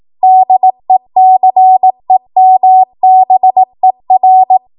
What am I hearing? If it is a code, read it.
Morse code, heard as a keyed tone, about 18 wpm, 750 Hz. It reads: DECEMBER
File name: Archive / 18 WPM / 18WPM